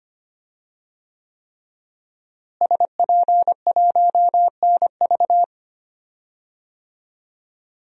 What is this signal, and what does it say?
Morse code, heard as a keyed tone, about 25 wpm, 700 Hz. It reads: SP1NV
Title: DXCC-100 Call Signs - 5 Characters - Repeated 3x - 25wpm